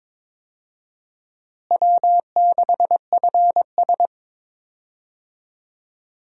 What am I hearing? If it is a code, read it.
Morse code, heard as a keyed tone, about 22 wpm, 700 Hz. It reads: W6FS
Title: USA Call Signs - 4 Characters - Repeated 2x - 22wpm